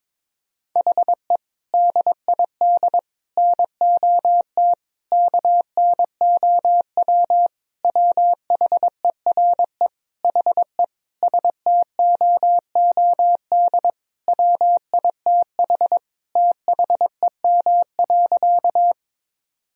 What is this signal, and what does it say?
Morse code, heard as a keyed tone, about 22 wpm, 700 Hz. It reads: HE DID NOT KNOW WHERE HE STOOD WITH THEM.
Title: Sentences from Top 1000 Words - Review Part 2 - 22wpm